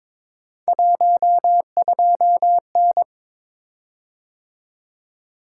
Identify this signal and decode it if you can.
Morse code, heard as a keyed tone, about 22 wpm, 700 Hz. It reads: 12N
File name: practice / 22wpm / Three Letter-Number Combinations - 22wpm